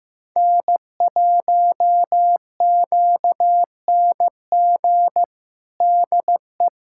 Morse code, heard as a keyed tone, about 16 wpm, 700 Hz. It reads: N1QNG DE
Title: USA Callsign DE Callsign - 5 Characters - 15wpm